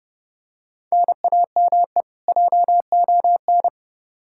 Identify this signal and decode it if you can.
Morse code, heard as a keyed tone, about 30 wpm, 700 Hz. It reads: NAME JON
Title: K1USN SST QSO Practice 30wpm